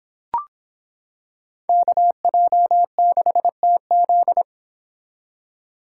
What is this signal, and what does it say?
Morse code, heard as a keyed tone, about 26 wpm, 700 Hz. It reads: KJ6TZ